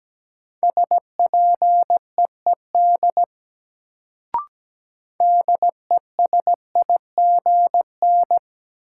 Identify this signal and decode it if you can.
Morse code, heard as a keyed tone, about 17 wpm, 700 Hz. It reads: SPEED DESIGN
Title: Sentences from Top 800 Words - New Word Drill - 17wpm